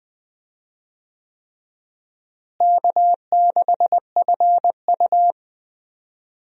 Morse code, heard as a keyed tone, about 20 wpm, 700 Hz. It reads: K6FU
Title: USA Call Signs - 4 Characters - Repeated 3x - 20wpm